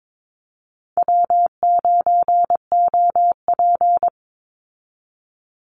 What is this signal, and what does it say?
Morse code, heard as a keyed tone, about 22 wpm, 700 Hz. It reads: W9OP
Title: Call Signs - 22wpm